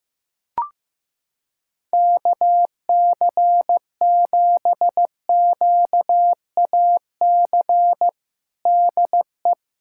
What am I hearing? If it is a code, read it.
Morse code, heard as a keyed tone, about 15 wpm, 700 Hz. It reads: KC7QAC DE
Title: USA Callsign DE Callsign - 6 Characters - 15wpm